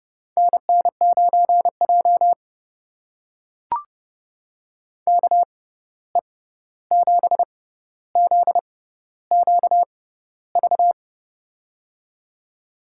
Morse code, heard as a keyed tone, about 30 wpm, spaced as for 14 wpm, 700 Hz. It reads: NN9J KE7ZQV
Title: USA Call Signs - 6x Character Spacing - 30wpm